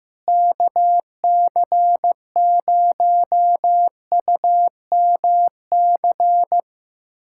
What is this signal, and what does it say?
Morse code, heard as a keyed tone, about 16 wpm, 700 Hz. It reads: KC0UMC
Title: USA Callsign DE Callsign - 6 Characters - 15wpm